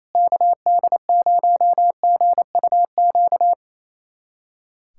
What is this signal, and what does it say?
Morse code, heard as a keyed tone, about 28 wpm, 700 Hz. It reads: KD0GUQ
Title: USA Callsign DE Callsign 28wpm